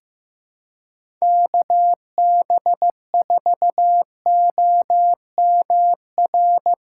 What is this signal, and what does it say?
Morse code, heard as a keyed tone, about 15 wpm, 700 Hz. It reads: KB4OMR
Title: USA Callsign DE Callsign - 6 Characters - 15wpm